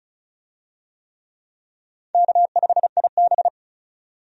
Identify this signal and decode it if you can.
Morse code, heard as a keyed tone, about 36 wpm, 700 Hz. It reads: K5IB
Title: USA Call Signs - 4 Characters - 35wpm